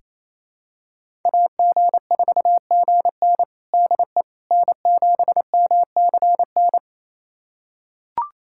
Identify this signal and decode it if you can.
Morse code, heard as a keyed tone, about 28 wpm, 700 Hz. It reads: AG4GN DE N7MCN
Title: USA Callsign DE Callsign - 5 Characters - 28wpm